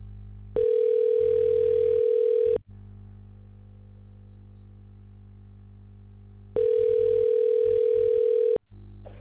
Ringback tone (North America), 440 + 480 Hz.